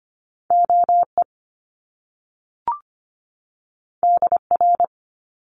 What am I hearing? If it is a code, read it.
Morse code, heard as a keyed tone, about 25 wpm, 700 Hz. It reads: OE DR
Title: Top Two Letter Combinations in Words 25wpm